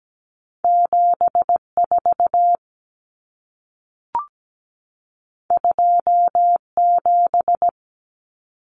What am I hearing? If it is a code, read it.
Morse code, heard as a keyed tone, about 17 wpm, 700 Hz. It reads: 74 27
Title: Common Cut Numbers - 2-Digit 17wpm